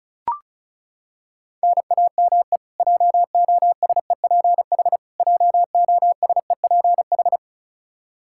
Morse code, heard as a keyed tone, about 36 wpm, 700 Hz. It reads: NAME JOSEPH JOSEPH
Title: Practice QSO - Exchange 1 - Beginning - 35wpm